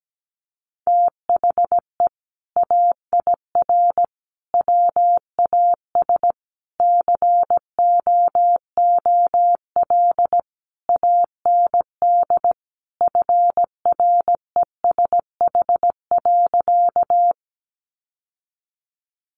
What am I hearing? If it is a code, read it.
Morse code, heard as a keyed tone, about 17 wpm, 700 Hz. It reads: THE AIR WAS COOL AND FRESH.
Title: Sentences from Top 1200 Words 17wpm